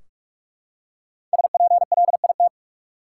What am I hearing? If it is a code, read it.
Morse code, heard as a keyed tone, about 43 wpm, 700 Hz. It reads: SPLIT
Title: Five Letter Words - 45wpm